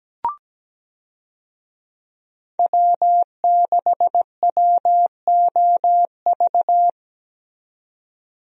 Morse code, heard as a keyed tone, about 17 wpm, 700 Hz. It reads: W6WOV
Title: USA Call Signs - 5 Characters - 17wpm